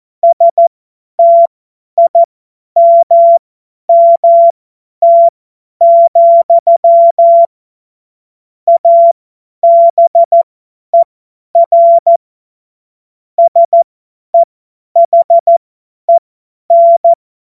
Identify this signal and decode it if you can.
Morse code, heard as a keyed tone, about 14 wpm, spaced as for 10 wpm, 680 Hz. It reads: STIMMT, ABER SEHEN
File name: Morsekurs / Witze / Witz 15. November 2022 50 F70